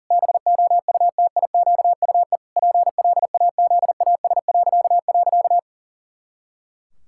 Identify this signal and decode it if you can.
Morse code, heard as a keyed tone, about 37 wpm, 700 Hz. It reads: BOUTIQUE PLAZAS..
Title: Pangram Challenge v2 - 40wpm